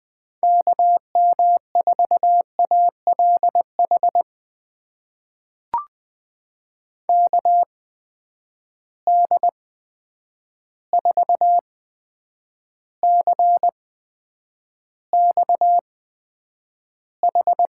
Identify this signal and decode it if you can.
Morse code, heard as a keyed tone, about 20 wpm, spaced as for 9 wpm, 700 Hz. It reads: KM4ALH KD4CXH
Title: USA Call Signs - 6 Characters - 8x Character Spacing - 20wpm